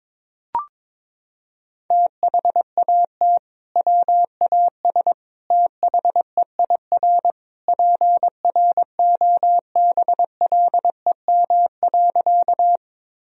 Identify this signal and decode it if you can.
Morse code, heard as a keyed tone, about 22 wpm, 700 Hz. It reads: THAT WAS THEIR PROBLEM.
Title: Sentences from Top 1000 Words - Review Part 2 - 22wpm